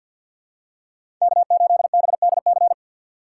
Morse code, heard as a keyed tone, about 51 wpm, 700 Hz. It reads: K8BDC
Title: USA Call Signs - 5 Characters - 2x Character Spacing - 50wpm